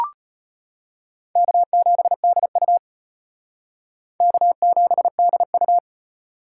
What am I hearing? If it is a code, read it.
Morse code, heard as a keyed tone, about 38 wpm, 700 Hz. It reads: K7DU K7DU